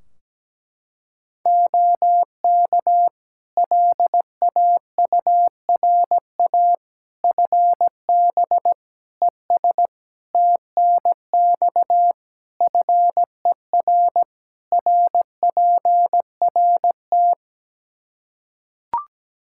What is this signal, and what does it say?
Morse code, heard as a keyed tone, about 17 wpm, 700 Hz. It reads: OK LAURA FB ES TNX FER RPRT
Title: K1USN SST QSO Practice 17wpm